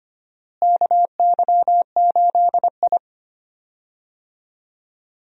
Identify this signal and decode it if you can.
Morse code, heard as a keyed tone, about 25 wpm, 700 Hz. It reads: KY8I
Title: USA Call Signs - 4 Characters - Repeated 3x - 25wpm